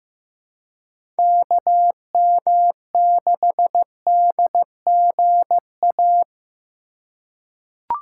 Morse code, heard as a keyed tone, about 15 wpm, 700 Hz. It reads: KM6DGA
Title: USA Call Signs - 6 Characters - 2x Character Spacing - 15wpm